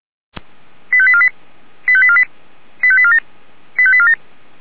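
Five-tone selective call (ZVEI), heard as quick runs of five tones, tones of about 70 ms. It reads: 85848 85848 85847 85847
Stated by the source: ZVEI five-tone call received by SDR, logged by Feuerwehr Orlamünde (fire brigade).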